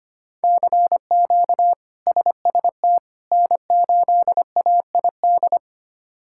Morse code, heard as a keyed tone, about 25 wpm, 700 Hz. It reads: CQ SST N8AID
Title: K1USN SST QSO Practice - 25wpm